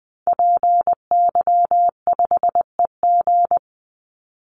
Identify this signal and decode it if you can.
Morse code, heard as a keyed tone, about 20 wpm, 700 Hz. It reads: PY5EG
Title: Call Signs - 20wpm